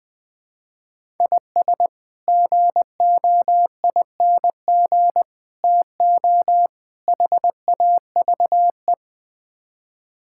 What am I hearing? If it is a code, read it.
Morse code, heard as a keyed tone, about 20 wpm, 700 Hz. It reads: IS GOING TO HAVE
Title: Build A Phrase - 5-Word Phrases - 20wpm